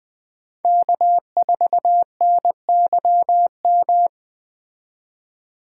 Morse code, heard as a keyed tone, about 20 wpm, 700 Hz. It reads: K4NYM
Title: 2023 POTA Activator Call Signs - 20wpm